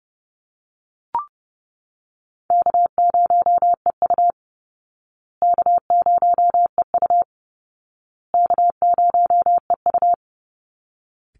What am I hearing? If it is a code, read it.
Morse code, heard as a keyed tone, about 30 wpm, 700 Hz. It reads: K0EU K0EU K0EU
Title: Call Signs Repeated 2x - 30wpm